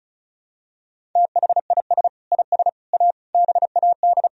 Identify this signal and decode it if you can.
Morse code, heard as a keyed tone, about 34 wpm, 700 Hz. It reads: THIS IS A BAD
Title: Sentences from Top 1000 Words - Review Part 1 - 35wpm